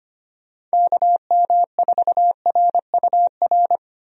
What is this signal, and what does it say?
Morse code, heard as a keyed tone, about 25 wpm, 700 Hz. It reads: KM4RUR
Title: USA Call Signs - 6 Characters - 25wpm